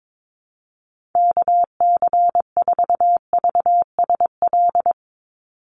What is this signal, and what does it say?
Morse code, heard as a keyed tone, about 22 wpm, 700 Hz. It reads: KC4VSL
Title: USA Call Signs 22wpm